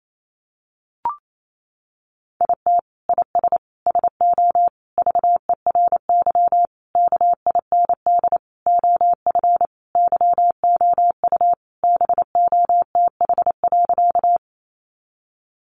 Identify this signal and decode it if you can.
Morse code, heard as a keyed tone, about 28 wpm, 700 Hz. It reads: IT IS SO VERY KIND OF YOU BOTH.
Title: Sentences from Top 300 Words 28wpm